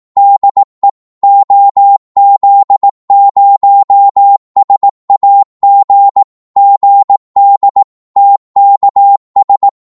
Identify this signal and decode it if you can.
Morse code, heard as a keyed tone, about 18 wpm, 800 Hz. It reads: DE OZ0SAG GD TKS
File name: Morse / qso-18wpm